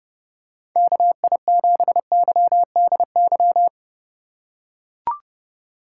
Morse code, heard as a keyed tone, about 30 wpm, 700 Hz. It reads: KI7YDY
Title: USA Call Signs - 6 Characters - 2x Character Spacing - 30wpm